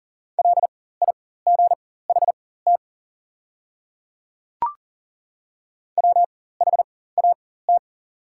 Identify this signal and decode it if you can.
Morse code, heard as a keyed tone, about 40 wpm, spaced as for 19 wpm, 700 Hz. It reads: LIGHT WHAT
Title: 4x Character Spacing - Top 300 Words - 40wpm